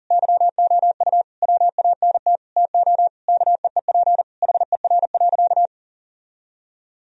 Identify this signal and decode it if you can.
Morse code, heard as a keyed tone, about 40 wpm, 700 Hz. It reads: YOU WANT TO KEEP HER.
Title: Sentences from Top 1000 Words - Review Part 2 - 40wpm